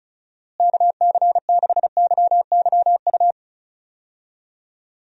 Morse code, heard as a keyed tone, about 36 wpm, 700 Hz. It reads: KC6YYU
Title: USA Call Signs - 6 Characters - Repeated 2x - 35wpm